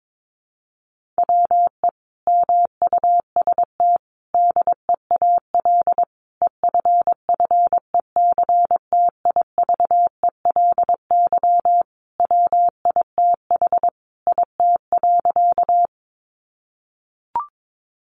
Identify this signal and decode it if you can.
Morse code, heard as a keyed tone, about 22 wpm, 700 Hz. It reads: WE MUST DEAL EFFECTIVELY WITH IT.
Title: Sentences from Top 1400 Words 22wpm